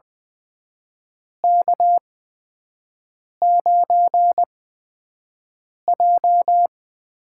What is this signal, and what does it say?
Morse code, heard as a keyed tone, about 20 wpm, spaced as for 5 wpm, 700 Hz. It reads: K9J
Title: USA Call Signs - 4 Characters - 8x Character Spacing - 20wpm